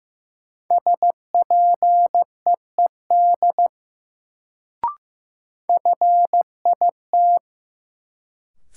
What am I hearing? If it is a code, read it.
Morse code, heard as a keyed tone, about 15 wpm, 700 Hz. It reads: SPEED FIT
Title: Sentences from Top 800 Words - New Word Drill - 15wpm